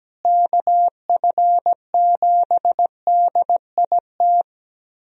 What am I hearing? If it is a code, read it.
Morse code, heard as a keyed tone, about 17 wpm, 700 Hz. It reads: KF7DIT
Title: USA Call Signs - 6 Characters - Repeated 3x - 17wpm